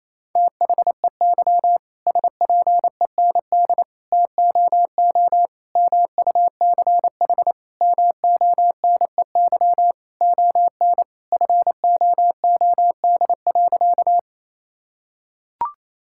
Morse code, heard as a keyed tone, about 28 wpm, 700 Hz. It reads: THEY SPEND TOO MUCH MONEY ON FOOD.
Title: Sentences from Top 1000 Words - Review Part 5 - 28wpm